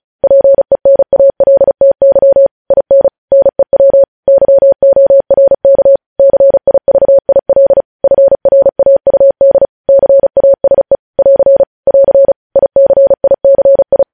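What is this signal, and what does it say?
Morse code, heard as a keyed tone, about 36 wpm, 550 Hz. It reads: PENALTY IN NEW YORK CIVIL FRAUD CASE <AR> <AR> ICICI
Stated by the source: News headlines sent in Morse at 35 WPM.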